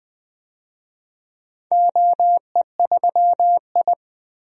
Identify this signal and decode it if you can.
Morse code, heard as a keyed tone, about 20 wpm, 700 Hz. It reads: OE3I
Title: DXCC-100 Call Signs - 4 Characters - 20wpm